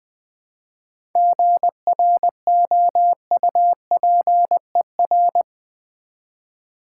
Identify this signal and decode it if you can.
Morse code, heard as a keyed tone, about 20 wpm, 700 Hz. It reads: GROUPER
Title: Difficult Words with Hints - 20wpm